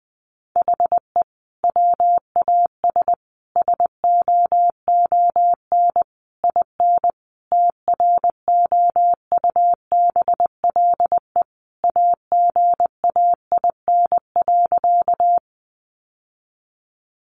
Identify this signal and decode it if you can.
Morse code, heard as a keyed tone, about 20 wpm, 700 Hz. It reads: HE WAS SOON IN TROUBLE AGAIN.
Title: Sentences from Top 1200 Words 20wpm